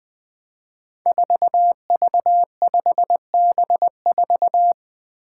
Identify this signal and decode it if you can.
Morse code, heard as a keyed tone, about 21 wpm, 700 Hz. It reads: 4V5B4
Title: Five Challenging Letter-Number Combinations 20wpm